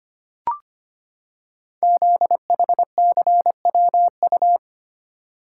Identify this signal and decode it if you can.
Morse code, heard as a keyed tone, about 25 wpm, 700 Hz. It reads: ZHCWU
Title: Code Groups - 5 Letters - 25wpm